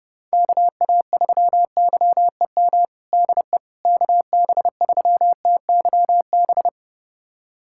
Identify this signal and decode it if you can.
Morse code, heard as a keyed tone, about 32 wpm, 700 Hz. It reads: KA3YEM DE KB3TYB
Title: USA Callsign DE Callsign 30wpm